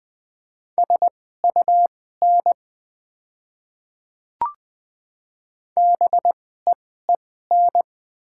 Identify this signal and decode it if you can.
Morse code, heard as a keyed tone, about 20 wpm, spaced as for 14 wpm, 700 Hz. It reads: SUN BEEN
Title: 2x Character Spacing - Top 300 Words - 20wpm